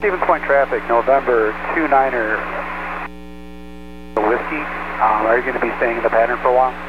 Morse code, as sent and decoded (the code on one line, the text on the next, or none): none